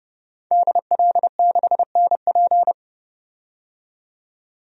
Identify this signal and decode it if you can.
Morse code, heard as a keyed tone, about 32 wpm, 700 Hz. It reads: DL6NP
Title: DXCC-100 Call Signs - 5 Characters - 30wpm